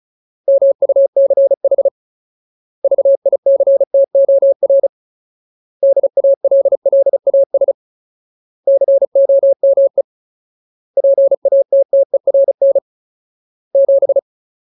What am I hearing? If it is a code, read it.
Morse code, heard as a keyed tone, about 34 wpm, 550 Hz. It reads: MUCH VICTOR DALLAS COME PATTERN 7
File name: Ultimate Word List / 35WPM Ultimate